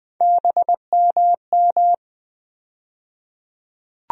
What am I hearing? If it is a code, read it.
Morse code, heard as a keyed tone, about 20 wpm, 700 Hz. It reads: BMM
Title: USA Call Sign Postfixes - 3 Letters 20wpm